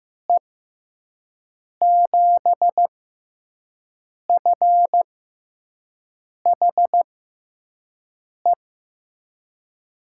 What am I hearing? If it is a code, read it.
Morse code, heard as a keyed tone, about 15 wpm, spaced as for 5 wpm, 700 Hz. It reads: E7FHE
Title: USA Call Signs - 6 Characters - 6x Character Spacing - 15wpm